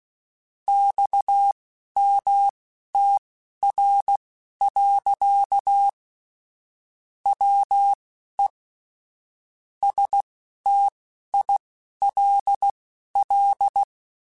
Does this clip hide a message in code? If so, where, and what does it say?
Yes: Morse code from 0.7 to 13.8 s, heard as a keyed tone, about 16 wpm, spaced as for 11 wpm, 800 Hz. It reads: XMTR. WE STILL